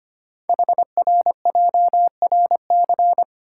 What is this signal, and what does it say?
Morse code, heard as a keyed tone, about 25 wpm, 700 Hz. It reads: HRJRC
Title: Code Groups - 5 Letters - 25wpm